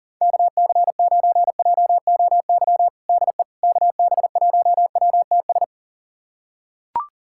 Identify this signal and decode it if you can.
Morse code, heard as a keyed tone, about 37 wpm, 700 Hz. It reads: KC9JOY DE KB1WTS
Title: USA Callsign DE Callsign - 6 Characters - 40wpm